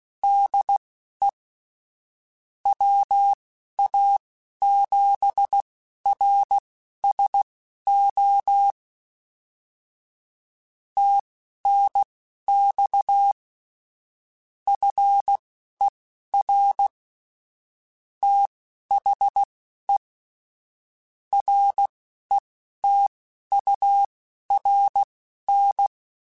Morse code, heard as a keyed tone, about 16 wpm, spaced as for 11 wpm, 800 Hz. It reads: DE WA7RSO TNX FER THE RETURN
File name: QSOs / QSOs-mp3s / kc7sac@10